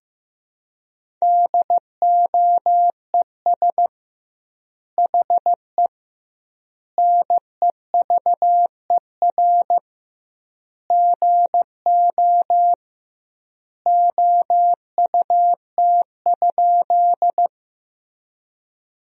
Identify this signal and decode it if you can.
Morse code, heard as a keyed tone, about 15 wpm, 700 Hz. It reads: DOES HE NEVER GO OUT?
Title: 2x Word-Spacing - Sentences from Top 300 Words - 15wpm